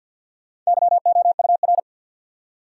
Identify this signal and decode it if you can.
Morse code, heard as a keyed tone, about 64 wpm, 700 Hz. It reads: YOUR